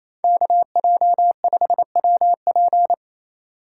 Morse code, heard as a keyed tone, about 28 wpm, 700 Hz. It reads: KJ5WP
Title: USA Call Signs - 5 Characters - 4x Character Spacing - 28wpm